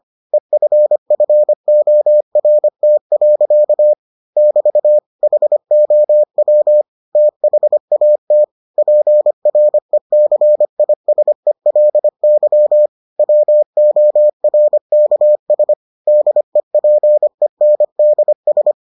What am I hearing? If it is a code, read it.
Morse code, heard as a keyed tone, about 25 wpm, 600 Hz. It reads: EFFORT. <BT> HOW THAT PRECISELY WORKS DEPENDS